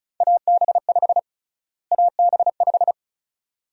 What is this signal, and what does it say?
Morse code, heard as a keyed tone, about 34 wpm, 700 Hz. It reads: AB5 AB5
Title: USA Call Sign Prefixes - Repeated 2x - 35wpm